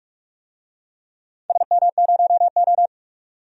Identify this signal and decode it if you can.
Morse code, heard as a keyed tone, about 42 wpm, 700 Hz. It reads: SM0O
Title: DXCC-100 Call Signs - 4 Characters - 45wpm